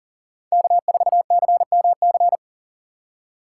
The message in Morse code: K4CMC

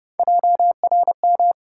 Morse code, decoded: JRM

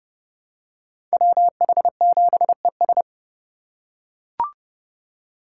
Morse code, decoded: WH7ES

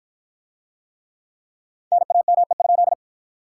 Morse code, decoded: NAME?

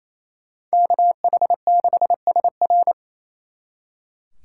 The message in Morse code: KH6SR

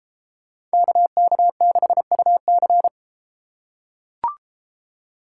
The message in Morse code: KK6UC

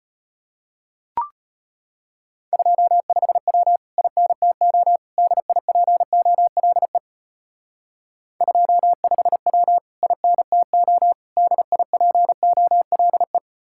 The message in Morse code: 25W INTO DIPOLE 25W INTO DIPOLE

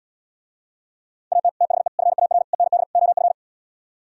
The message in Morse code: AT THE FAR END OF

